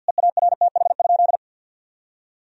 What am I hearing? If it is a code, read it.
Morse code, heard as a keyed tone, about 47 wpm, 700 Hz. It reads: EARTH?
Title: Sentences from Top 1000 Words - Review Part 2 - 50wpm